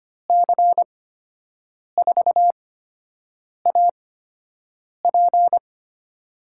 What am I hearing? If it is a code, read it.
Morse code, heard as a keyed tone, about 25 wpm, spaced as for 7 wpm, 700 Hz. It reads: C4AP
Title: USA Call Signs - 5 Characters - 8x Character Spacing - 25wpm